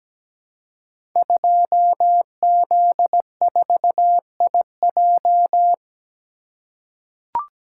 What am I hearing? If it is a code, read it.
Morse code, heard as a keyed tone, about 17 wpm, 700 Hz. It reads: 2Z4IJ
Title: Code Groups - 5 Letter:Numbers - 17wpm